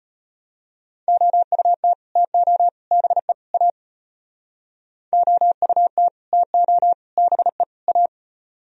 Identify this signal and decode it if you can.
Morse code, heard as a keyed tone, about 32 wpm, 700 Hz. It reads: OUT TO BE A OUT TO BE A